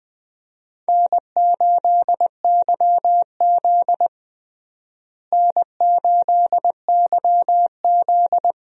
Morse code, heard as a keyed tone, about 20 wpm, 700 Hz. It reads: N8YZ N8YZ
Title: USA Call Signs - 4 Characters - Repeated 3x - 20wpm